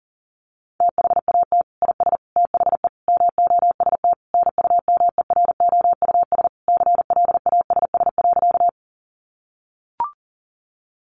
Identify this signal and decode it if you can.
Morse code, heard as a keyed tone, about 37 wpm, 700 Hz. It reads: THAT IS THE MOST NUMEROUS CLASS.
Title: Sentences from Top 1400 Words 40wpm